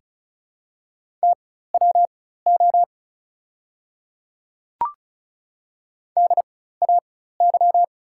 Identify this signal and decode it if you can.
Morse code, heard as a keyed tone, about 36 wpm, spaced as for 16 wpm, 700 Hz. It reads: TWO DAY
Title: 4x Character Spacing - Top 100 Words - 35wpm